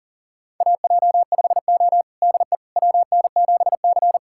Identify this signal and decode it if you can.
Morse code, heard as a keyed tone, about 37 wpm, 700 Hz. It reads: AJ5O DE WN7C
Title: USA Callsign DE Callsign - 4 Characters - 40wpm